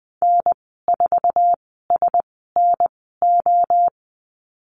Morse code, heard as a keyed tone, about 20 wpm, spaced as for 15 wpm, 700 Hz. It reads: N4SNO
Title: 055 Lesson 10g - Letter r - Call Signs 20wpm